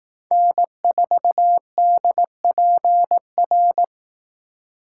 Morse code, heard as a keyed tone, about 18 wpm, 700 Hz. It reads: N4DPR
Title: Speed Racing - USA 1x3 Call Signs 15wpm